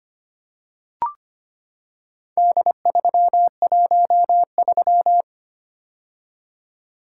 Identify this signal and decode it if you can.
Morse code, heard as a keyed tone, about 25 wpm, 700 Hz. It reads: D313
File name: practice / 25wpm / Four Letter-Number Combinations - 25wpm